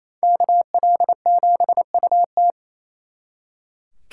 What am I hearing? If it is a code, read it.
Morse code, heard as a keyed tone, about 32 wpm, 700 Hz. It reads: KL7UT
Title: Speed Racing - USA 2x2 Call Signs 28wpm